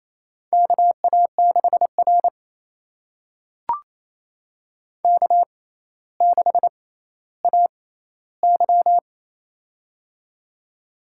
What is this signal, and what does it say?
Morse code, heard as a keyed tone, about 27 wpm, spaced as for 14 wpm, 700 Hz. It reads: KA6R K6AY
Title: USA Call Signs - 4 Characters - 6x Character Spacing - 28wpm